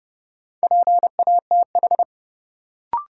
Morse code, heard as a keyed tone, about 30 wpm, 700 Hz. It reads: PATH
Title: Sentences from Top 1400 Words - New Word Drill - 30wpm